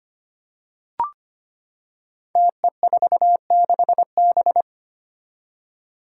Morse code, heard as a keyed tone, about 25 wpm, 700 Hz. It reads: TE46B
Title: Five Challenging Letter-Number Combinations 25wpm